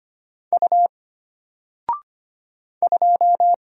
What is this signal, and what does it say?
Morse code, heard as a keyed tone, about 25 wpm, 700 Hz. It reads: U 2
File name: Single-Letter-Number-25wpm